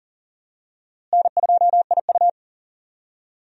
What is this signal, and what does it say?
Morse code, heard as a keyed tone, about 40 wpm, 700 Hz. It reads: N2IU
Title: USA Call Signs - 4 Characters - Repeated 2x - 40wpm